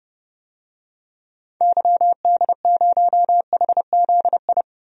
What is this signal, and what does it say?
Morse code, heard as a keyed tone, about 30 wpm, 700 Hz. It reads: YD0HZI
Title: DXCC-100 Call Signs - 6 Characters - Repeated 3x - 30wpm